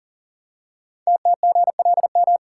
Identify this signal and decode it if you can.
Morse code, heard as a keyed tone, about 42 wpm, 700 Hz. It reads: TTGLM
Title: Code Groups - 5 Letters - 40wpm